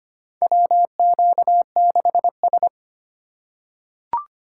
Morse code, heard as a keyed tone, about 25 wpm, 700 Hz. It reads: WQ6S
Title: USA Call Signs - 4 Characters - 4x Character Spacing - 25wpm